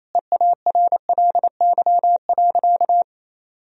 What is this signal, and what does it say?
Morse code, heard as a keyed tone, about 28 wpm, 700 Hz. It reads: EARLY.
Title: Sentences from Top 1000 Words - Review Part 4 - 28wpm